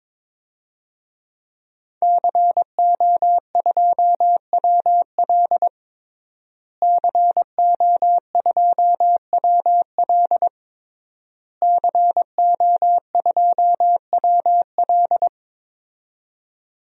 Morse code, heard as a keyed tone, about 22 wpm, 700 Hz. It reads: CO2WL CO2WL CO2WL
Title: DXCC-100 Call Signs - 5 Characters - Repeated 3x - 22wpm